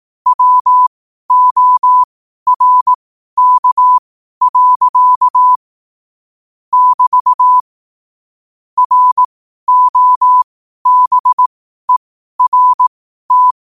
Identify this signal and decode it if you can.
Morse code, heard as a keyed tone, about 18 wpm, spaced as for 12 wpm, 1000 Hz. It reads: WORK. <BT> ROBERT
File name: qotdfiles / 12wpm